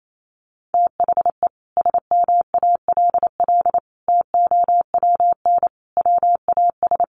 Morse code, heard as a keyed tone, about 28 wpm, 700 Hz. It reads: THE SMALL TOWN WAS
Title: Sentences from Top 1300 Words 28wpm